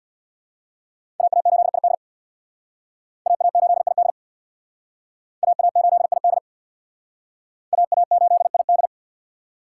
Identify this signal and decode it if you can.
Morse code, heard as a keyed tone, about 60 wpm, 700 Hz. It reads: AA8ID AA8ID AA8ID AA8ID
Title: Speed Racing - USA 2x2 Call Signs 50wpm